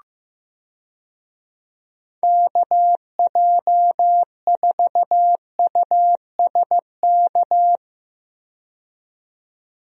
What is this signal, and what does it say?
Morse code, heard as a keyed tone, about 15 wpm, 700 Hz. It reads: KJ4USK
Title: USA Call Signs - 6 Characters - 15wpm